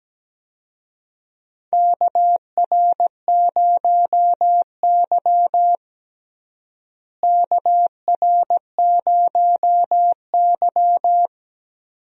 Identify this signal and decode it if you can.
Morse code, heard as a keyed tone, about 17 wpm, 700 Hz. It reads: KR0Y KR0Y
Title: USA Call Signs - 4 Characters - Repeated 2x - 17wpm